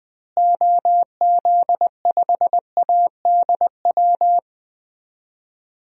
Morse code, heard as a keyed tone, about 20 wpm, 700 Hz. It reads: OZ5ADW
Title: DXCC-100 Call Signs - 6 Characters - Repeated 3x - 20wpm